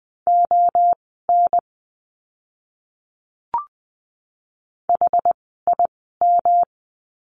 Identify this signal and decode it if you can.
Morse code, heard as a keyed tone, about 20 wpm, spaced as for 14 wpm, 700 Hz. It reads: ON HIM
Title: 237 Lesson 38f - Letter q - Words 20wpm